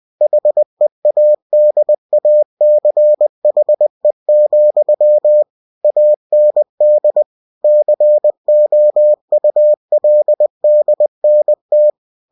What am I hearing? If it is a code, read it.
Morse code, heard as a keyed tone, about 20 wpm, 600 Hz. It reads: HEADACHE, AND COULDNT
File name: cw-books / winnie-the-pooh / winnie-20wpm-page-0041